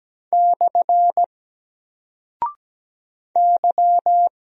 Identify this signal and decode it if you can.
Morse code, heard as a keyed tone, about 17 wpm, 700 Hz. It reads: / Y
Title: Single Character (Letter:Number:Punctuation:Prosign) 17wpm